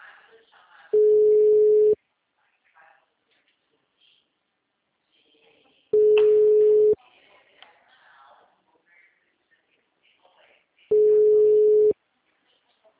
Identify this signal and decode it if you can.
Ringback tone (Europe, 425 Hz), with speech.
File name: out- -2029- - - .2271